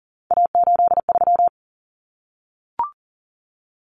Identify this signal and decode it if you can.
Morse code, heard as a keyed tone, about 40 wpm, 700 Hz. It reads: A83
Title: Common Cut Numbers - 3-Digit 40wpm